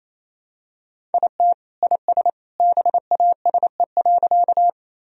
Morse code, heard as a keyed tone, about 28 wpm, 700 Hz. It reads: IT IS BASE.
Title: Sentences from Top 1000 Words - Review Part 1 - 28wpm